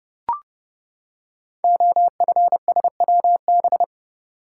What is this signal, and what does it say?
Morse code, heard as a keyed tone, about 30 wpm, 700 Hz. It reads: OFSWB
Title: Code Groups - 5 Letters - 30wpm